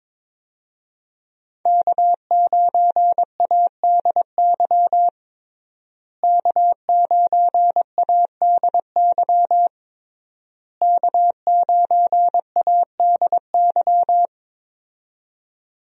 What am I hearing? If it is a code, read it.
Morse code, heard as a keyed tone, about 22 wpm, 700 Hz. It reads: K9ADY K9ADY K9ADY
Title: USA Call Signs - 5 Characters - Repeated 3x - 22wpm